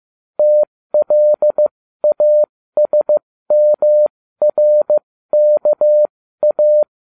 Morse code, heard as a keyed tone, about 15 wpm, spaced as for 13 wpm, 600 Hz. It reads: TLASMRKA